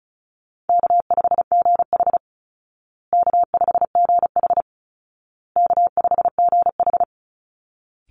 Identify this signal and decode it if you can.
Morse code, heard as a keyed tone, about 34 wpm, 700 Hz. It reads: K5GH K5GH K5GH
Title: Easy Call Signs Repeated 3x - 35wpm